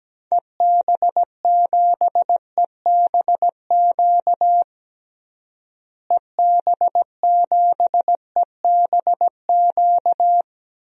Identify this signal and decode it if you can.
Morse code, heard as a keyed tone, about 17 wpm, 700 Hz. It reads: EB7EBQ EB7EBQ
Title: DXCC-100 Call Signs - 6 Characters - Repeated 3x - 17wpm